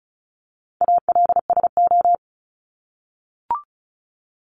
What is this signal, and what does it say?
Morse code, heard as a keyed tone, about 36 wpm, 700 Hz. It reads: ALSO
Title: Top 200 Words 35wpm